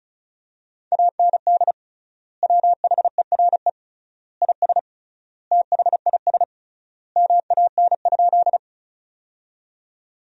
Morse code, heard as a keyed tone, about 34 wpm, 700 Hz. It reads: AND WHERE IS THIS MAN?
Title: 3x Word-Spacing - Sentences from Top 200 Words - 35wpm